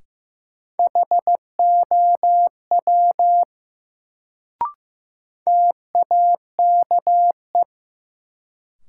Morse code, heard as a keyed tone, about 15 wpm, 700 Hz. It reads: HOW TAKE
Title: Sets of 2 Words - New Word Drill - 15wpm